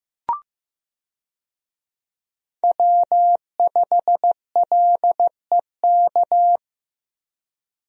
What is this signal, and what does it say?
Morse code, heard as a keyed tone, about 15 wpm, 700 Hz. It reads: W5LEK
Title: USA Call Signs - 5 Characters - Repeated 3x - 15wpm